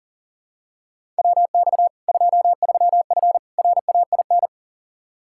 Morse code, heard as a keyed tone, about 38 wpm, 700 Hz. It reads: WX 23F RAIN
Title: Practice QSO - Exchange 2 - Info - 40wpm